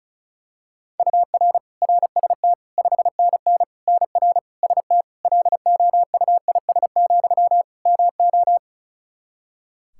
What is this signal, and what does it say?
Morse code, heard as a keyed tone, about 36 wpm, 700 Hz. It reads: UR RST 5NN NR ST LOUIS, MO
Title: Practice QSO - Exchange 1 - Beginning - 35wpm